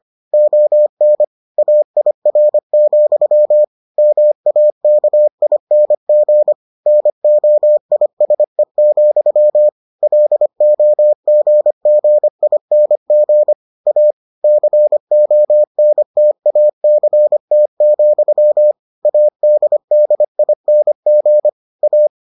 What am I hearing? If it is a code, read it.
Morse code, heard as a keyed tone, about 25 wpm, 600 Hz. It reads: ON AIR, MAKING NOISE, LOGGING A CONTACT, ADDING A